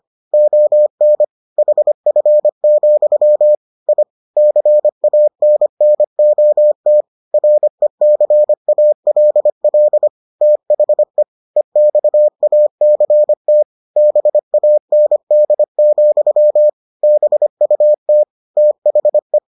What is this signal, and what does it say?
Morse code, heard as a keyed tone, about 25 wpm, 600 Hz. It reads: ON HF, I CANNOT RECALL THE EXACT BAND, BUT THE